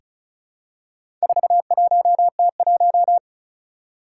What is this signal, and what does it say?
Morse code, heard as a keyed tone, about 36 wpm, 700 Hz. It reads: 41T1
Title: Four Challenging Letter-Number Combinations 35wpm